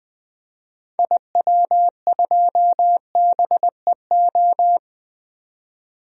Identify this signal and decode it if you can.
Morse code, heard as a keyed tone, about 20 wpm, 700 Hz. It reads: IW2BEO
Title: DXCC-100 Call Signs - 6 Characters - 20wpm